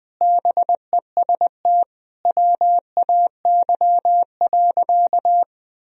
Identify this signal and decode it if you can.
Morse code, heard as a keyed tone, about 20 wpm, 700 Hz. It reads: BEST WAY.
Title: Sentences from Top 400 Words and Abbreviations 20wpm